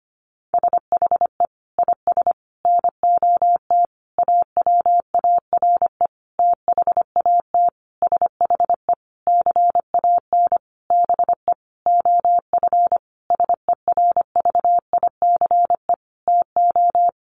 Morse code, heard as a keyed tone, about 25 wpm, 700 Hz. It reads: SHE IS NOT AWARE THAT SHE CAN BE OF SERVICE TO
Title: Sentences from Top 1200 Words 25wpm